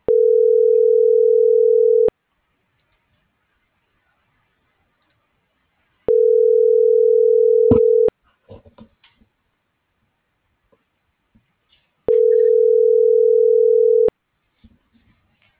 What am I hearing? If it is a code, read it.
Ringback tone (North America), 440 + 480 Hz.